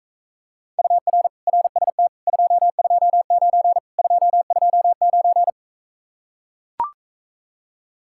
Morse code, heard as a keyed tone, about 44 wpm, 700 Hz. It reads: UR RST 229 229